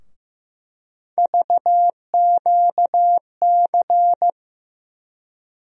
Morse code, heard as a keyed tone, about 15 wpm, 700 Hz. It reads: VQC
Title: USA Call Sign Postfixes - 3 Letters 15wpm